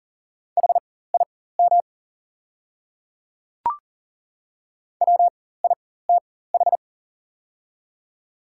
Morse code, heard as a keyed tone, about 38 wpm, spaced as for 18 wpm, 690 Hz. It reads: HIM WITH
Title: 4x Character Spacing - Top 100 Words - 40wpm